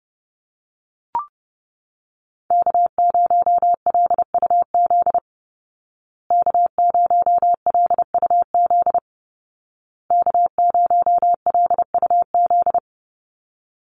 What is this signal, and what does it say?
Morse code, heard as a keyed tone, about 30 wpm, 700 Hz. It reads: K0LUZ K0LUZ K0LUZ
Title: Call Signs Repeated 2x - 30wpm